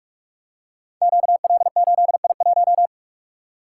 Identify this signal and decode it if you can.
Morse code, heard as a keyed tone, about 46 wpm, 700 Hz. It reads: QL8I1
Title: Code Groups - 5 Letter:Numbers - 45wpm